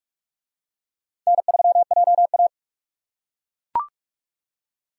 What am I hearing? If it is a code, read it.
Morse code, heard as a keyed tone, about 42 wpm, 700 Hz. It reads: N3JA